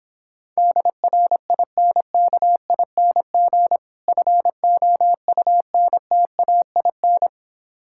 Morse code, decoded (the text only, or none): DRINKING FOUNTAIN